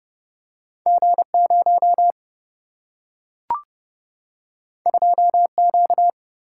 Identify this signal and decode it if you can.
Morse code, heard as a keyed tone, about 30 wpm, 700 Hz. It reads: G0 2Q
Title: Single-Letter Single-Number 30wpm